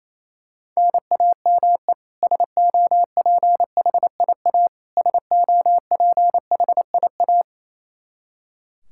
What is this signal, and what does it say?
Morse code, heard as a keyed tone, about 28 wpm, 700 Hz. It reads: NAME SOPHIA SOPHIA
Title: Practice QSO - Exchange 1 - Beginning - 28wpm